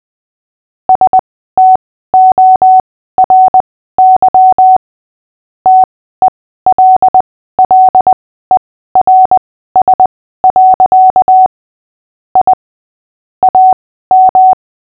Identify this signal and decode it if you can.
Morse code, heard as a keyed tone, about 20 wpm, spaced as for 14 wpm, 750 Hz. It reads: STORY TELLERS. I AM
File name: cw / fists / Short-Story-8-14wpm